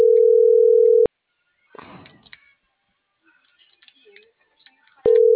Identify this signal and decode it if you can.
Ringback tone (North America), 440 + 480 Hz.